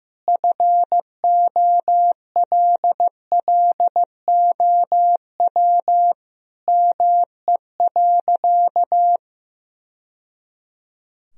Morse code, heard as a keyed tone, about 15 wpm, 700 Hz. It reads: FOLLOW ME.
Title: Sentences from Top 1000 Words - Review Part 2 - 15wpm